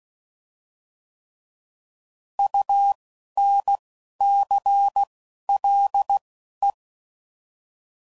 Morse code, heard as a keyed tone, about 16 wpm, spaced as for 12 wpm, 800 Hz. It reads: UNCLE
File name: Prefixes / By First Letter / Us